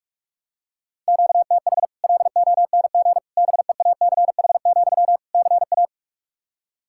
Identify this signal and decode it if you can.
Morse code, heard as a keyed tone, about 45 wpm, 700 Hz. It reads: QTH LONG BEACH, CA
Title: Practice QSO - Exchange 1 - Beginning - 45wpm